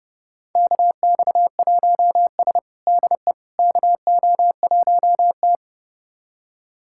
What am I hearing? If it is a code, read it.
Morse code, heard as a keyed tone, about 30 wpm, 700 Hz. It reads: KX1S DE KO1T
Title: USA Callsign DE Callsign - 4 Characters - 30wpm